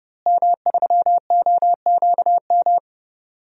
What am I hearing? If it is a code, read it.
Morse code, heard as a keyed tone, about 30 wpm, 700 Hz. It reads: M3OQM
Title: DXCC-100 Call Signs - 5 Characters - Repeated 3x - 30wpm